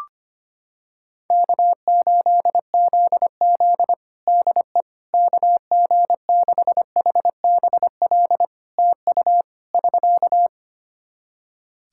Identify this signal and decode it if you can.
Morse code, heard as a keyed tone, about 25 wpm, 700 Hz. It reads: K8ZZ DE KG6HBL TU <SK>
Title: Practice QSO - Exchange 4 - Ending - 25wpm